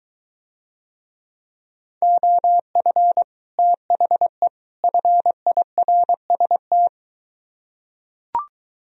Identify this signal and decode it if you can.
Morse code, heard as a keyed tone, about 22 wpm, 700 Hz. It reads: OF THE FIRST